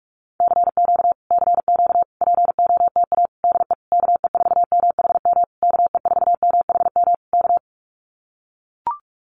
Morse code, heard as a keyed tone, about 44 wpm, 700 Hz. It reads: CQ CQ POTA DE KE4MHM KE4MHM K
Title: Common QSO Phrases Encore 45wpm